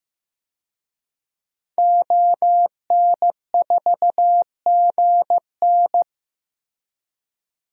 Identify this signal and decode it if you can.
Morse code, heard as a keyed tone, about 15 wpm, 700 Hz. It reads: ON4GN
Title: DXCC-100 Call Signs - 5 Characters - 15wpm